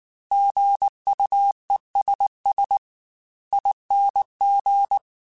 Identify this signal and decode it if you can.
Morse code, heard as a keyed tone, about 19 wpm, 800 Hz. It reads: GUESS ING